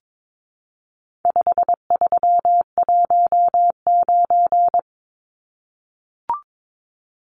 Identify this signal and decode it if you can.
Morse code, heard as a keyed tone, about 22 wpm, 700 Hz. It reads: 5319
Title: Numbers - 4-Digit 22wpm